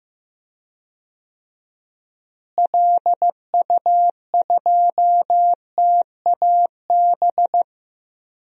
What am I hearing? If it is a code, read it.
Morse code, heard as a keyed tone, about 15 wpm, 700 Hz. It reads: LU2TAB
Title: DXCC-100 Call Signs - 6 Characters - Repeated 2x - 15wpm